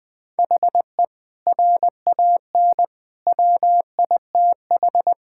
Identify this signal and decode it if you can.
Morse code, heard as a keyed tone, about 20 wpm, 700 Hz. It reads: HE RAN WITH
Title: Sentences from Top 1000 Words - Review Part 5 - 20wpm